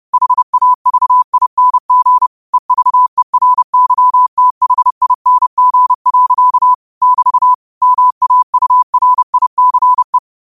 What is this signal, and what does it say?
Morse code, heard as a keyed tone, about 30 wpm, 1000 Hz. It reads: HAVING EVERYTHING. <BT> MAURICE
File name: qotdfiles / 30wpm